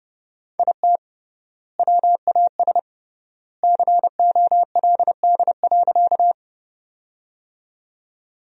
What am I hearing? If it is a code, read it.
Morse code, heard as a keyed tone, about 30 wpm, 700 Hz. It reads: IT WAS COLD.
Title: 3x Word-Spacing - Sentences from Top 400 Words - 30wpm